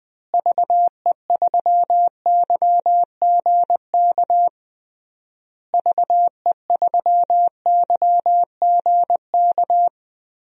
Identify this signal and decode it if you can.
Morse code, heard as a keyed tone, about 21 wpm, 700 Hz. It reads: VE3YGK VE3YGK
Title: DXCC-100 Call Signs - 6 Characters - Repeated 3x - 20wpm